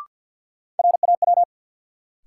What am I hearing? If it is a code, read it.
Morse code, heard as a keyed tone, about 52 wpm, 700 Hz. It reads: RAW
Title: Sentences from Top 1500 Words - New Word Drill - 50wpm